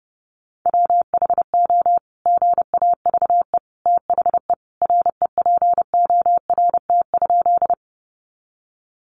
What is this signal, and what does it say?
Morse code, heard as a keyed tone, about 30 wpm, 700 Hz. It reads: WHO GAVE THE REPORT?
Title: Sentences from Top 900 Words 30wpm